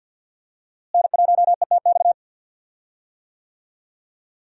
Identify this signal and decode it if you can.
Morse code, heard as a keyed tone, about 53 wpm, 700 Hz. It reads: N1ETX
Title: USA Call Signs - 5 Characters - Repeated 3x - 50wpm